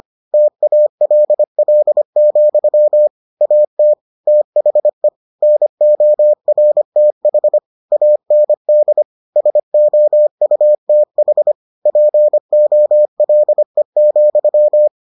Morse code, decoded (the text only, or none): TALL, AT THE NORTH AND SOUTH POLE,